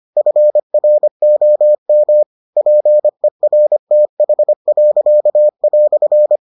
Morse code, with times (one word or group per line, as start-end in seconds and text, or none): 0.2-2.2 FROM
2.6-6.4 PERTH."